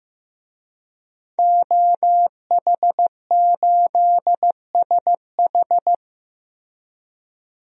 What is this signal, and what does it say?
Morse code, heard as a keyed tone, about 15 wpm, 700 Hz. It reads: OH8SH
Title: DXCC-100 Call Signs - 5 Characters - Repeated 3x - 15wpm